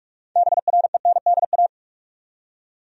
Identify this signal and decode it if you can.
Morse code, heard as a keyed tone, about 46 wpm, 700 Hz. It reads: BRENDA
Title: USA Common Names - 45wpm